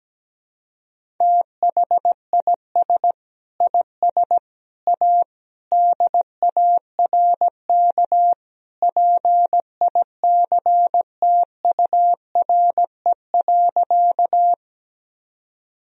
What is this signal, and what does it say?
Morse code, heard as a keyed tone, about 17 wpm, 700 Hz. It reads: THIS IS A DARK PICTURE.
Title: Sentences from Top 1000 Words - Review Part 1 - 17wpm